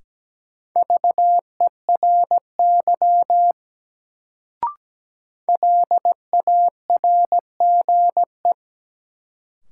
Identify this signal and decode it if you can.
Morse code, heard as a keyed tone, about 17 wpm, 700 Hz. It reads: VERY LARGE
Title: Sentences from Top 200 Words - New Word Drill - 17wpm